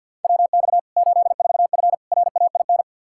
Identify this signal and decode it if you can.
Morse code, heard as a keyed tone, about 46 wpm, 700 Hz. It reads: WX 84F RAIN
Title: K1USN SST QSO Practice 50wpm